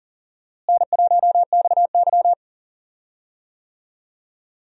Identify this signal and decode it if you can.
Morse code, heard as a keyed tone, about 40 wpm, 700 Hz. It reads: N1XY
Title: USA Call Signs - 4 Characters - Repeated 3x - 40wpm